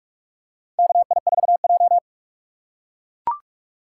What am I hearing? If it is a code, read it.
Morse code, heard as a keyed tone, about 41 wpm, 700 Hz. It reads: KI4J
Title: USA Call Signs - 2x Character Spacing - 45wpm